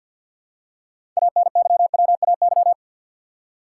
Morse code, heard as a keyed tone, about 52 wpm, 700 Hz. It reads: ANYWAY
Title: Six Letter Words - 50wpm